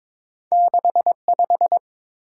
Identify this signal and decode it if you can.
Morse code, heard as a keyed tone, about 30 wpm, 700 Hz. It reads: 6 5